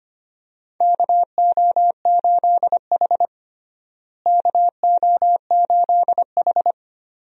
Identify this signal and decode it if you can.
Morse code, heard as a keyed tone, about 25 wpm, 700 Hz. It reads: KO8H KO8H
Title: USA Call Signs - 4 Characters - Repeated 2x - 25wpm